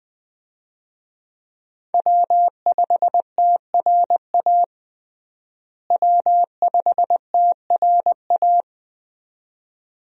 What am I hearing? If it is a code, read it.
Morse code, heard as a keyed tone, about 21 wpm, 700 Hz. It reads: W5TRA W5TRA
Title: USA Call Signs - 5 Characters - Repeated 2x - 20wpm